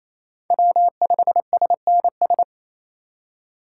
Morse code, heard as a keyed tone, about 28 wpm, 700 Hz. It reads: W5SNS
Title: USA Call Signs - 8x Character Spacing - 28wpm